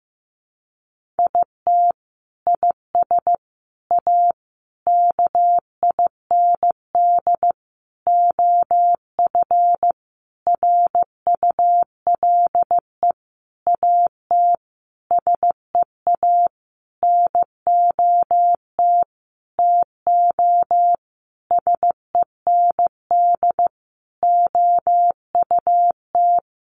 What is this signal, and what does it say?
Morse code, heard as a keyed tone, about 15 wpm, 700 Hz. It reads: IT IS A KIND OF RULE AT SEA NOT TO SEND OUT
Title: Sentences from Top 1500 Words 15wpm